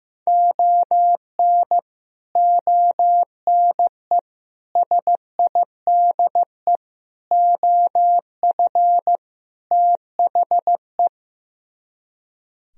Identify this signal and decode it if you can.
Morse code, heard as a keyed tone, about 15 wpm, 700 Hz. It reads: ON ONE SIDE OF THE
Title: Speed-Racing - Sets of 5 Words 15wpm